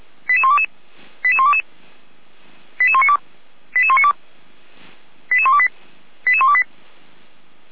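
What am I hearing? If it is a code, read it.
Five-tone selective call (ZVEI), heard as quick runs of five tones, tones of about 70 ms. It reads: 88122 88122 88192 88192 88128 88128
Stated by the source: ZVEI five-tone selective call received by SDR.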